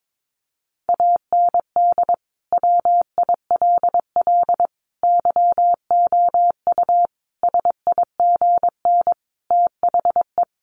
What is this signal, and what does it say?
Morse code, heard as a keyed tone, about 22 wpm, 700 Hz. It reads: AND WILL YOU SIGN THE
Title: Sentences from Top 1400 Words 22wpm